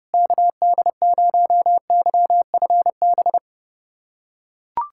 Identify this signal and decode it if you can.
Morse code, heard as a keyed tone, about 28 wpm, 700 Hz. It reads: KD0YFB
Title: USA Callsign DE Callsign - 6 Characters - 30wpm